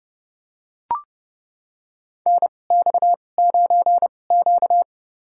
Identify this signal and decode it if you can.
Morse code, heard as a keyed tone, about 30 wpm, spaced as for 22 wpm, 690 Hz. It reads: NX9Q
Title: USA Call Signs - 4 Characters - 2x Character Spacing - 30wpm